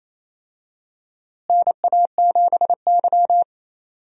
Morse code, heard as a keyed tone, about 28 wpm, 700 Hz. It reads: NA7Y
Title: USA Call Signs - 4 Characters - Repeated 3x - 28wpm